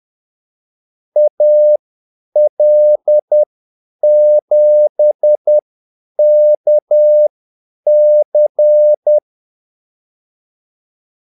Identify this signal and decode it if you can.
Morse code, heard as a keyed tone, about 10 wpm, spaced as for 8 wpm, 600 Hz. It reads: AL7KC
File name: Callsigns-10-8wpm